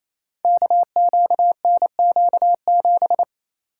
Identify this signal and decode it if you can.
Morse code, heard as a keyed tone, about 28 wpm, 700 Hz. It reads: KQNQ7
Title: Code Groups - 5 Letter:Numbers - 28wpm